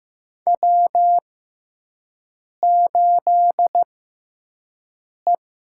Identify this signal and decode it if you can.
Morse code, heard as a keyed tone, about 15 wpm, spaced as for 5 wpm, 700 Hz. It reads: W8E
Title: USA Call Signs - 6x Character Spacing - 15wpm